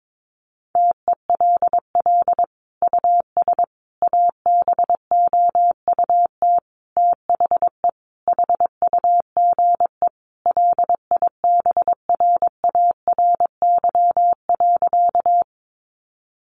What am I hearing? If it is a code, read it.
Morse code, heard as a keyed tone, about 22 wpm, 700 Hz. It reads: TELL US ABOUT THE HUGE LIBRARY.
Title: Sentences from Top 1100 Words 22wpm